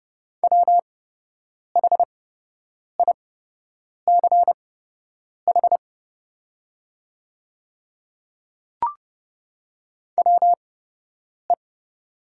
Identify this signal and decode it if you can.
Morse code, heard as a keyed tone, about 30 wpm, spaced as for 8 wpm, 700 Hz. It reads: WHICH WE
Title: 8x Character Spacing - Top 100 Words - 30wpm